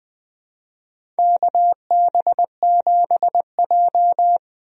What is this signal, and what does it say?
Morse code, heard as a keyed tone, about 20 wpm, 700 Hz. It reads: KB7J
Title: USA Call Signs - 4 Characters - 20wpm